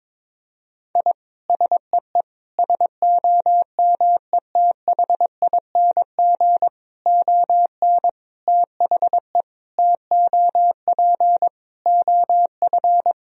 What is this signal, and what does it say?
Morse code, heard as a keyed tone, about 22 wpm, 700 Hz. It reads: I SEE SOMETHING ON THE TOP OF
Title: Sentences from Top 1000 Words - Review Part 2 - 22wpm